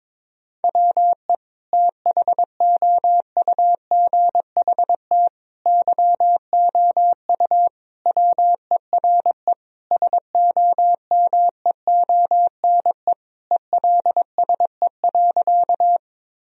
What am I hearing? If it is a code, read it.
Morse code, heard as a keyed tone, about 22 wpm, 700 Hz. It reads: WE THOUGHT YOU WERE SOMEONE ELSE.
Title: Sentences from Top 1000 Words - Review Part 4 - 22wpm